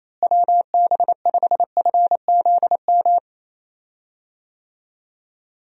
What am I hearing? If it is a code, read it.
Morse code, heard as a keyed tone, about 28 wpm, 700 Hz. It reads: WB5FZM
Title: USA Call Signs - 6 Characters - Repeated 3x - 28wpm